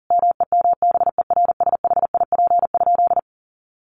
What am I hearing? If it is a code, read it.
Morse code, heard as a keyed tone, about 40 wpm, 700 Hz. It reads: MEMBERSHIP?
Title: Sentences from Top 1400 Words 40wpm